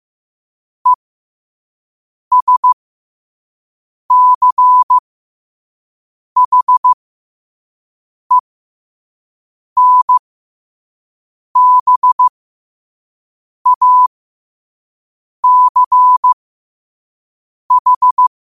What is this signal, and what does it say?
Morse code, heard as a keyed tone, about 15 wpm, spaced as for 5 wpm, 1000 Hz. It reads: ESCHENBACH
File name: qotdfiles / 5wpm